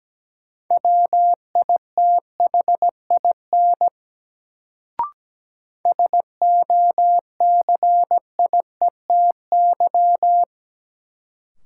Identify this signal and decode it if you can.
Morse code, heard as a keyed tone, about 17 wpm, 700 Hz. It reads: WITHIN SOCIETY
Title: Sentences from Top 700 Words - New Word Drill - 17wpm